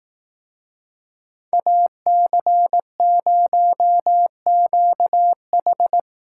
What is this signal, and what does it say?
Morse code, heard as a keyed tone, about 18 wpm, 700 Hz. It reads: AC0QH
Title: Speed Racing - USA 2x2 Call Signs 15wpm